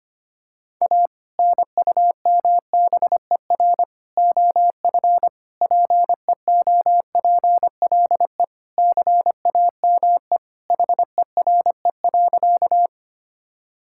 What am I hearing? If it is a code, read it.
Morse code, heard as a keyed tone, about 25 wpm, 700 Hz. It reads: A NUMBER OF PEOPLE CAME HERE.
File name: practice / 25wpm / Sentences with 2-Word Phrases - 25wpm